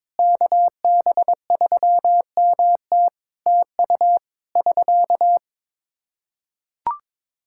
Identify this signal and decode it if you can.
Morse code, heard as a keyed tone, about 22 wpm, 700 Hz. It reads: KB3MT TU <SK>
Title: Practice QSO - Exchange 4 - Ending - 22wpm